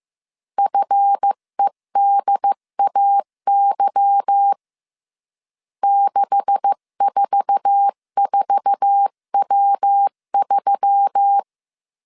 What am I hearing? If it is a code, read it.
Morse code, heard as a keyed tone, about 15 wpm, 800 Hz. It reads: FEDAY 644W3
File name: CW / DET / 1 9 1500C S B 14wpm 3302